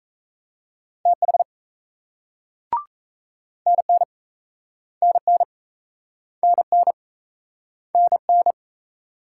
Morse code, heard as a keyed tone, about 38 wpm, 700 Hz. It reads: TH NN NN NN NN